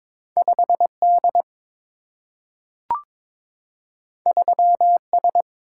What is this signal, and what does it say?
Morse code, heard as a keyed tone, about 22 wpm, 700 Hz. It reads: 5 D 3 S
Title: Single-Letter Single-Number 22wpm